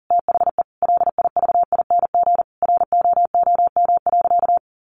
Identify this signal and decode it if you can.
Morse code, heard as a keyed tone, about 40 wpm, 700 Hz. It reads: THE LIVING ROOM.
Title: Sentences from Top 1200 Words 40wpm